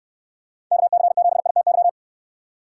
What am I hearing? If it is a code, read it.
Morse code, heard as a keyed tone, about 69 wpm, 700 Hz. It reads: KC7ITY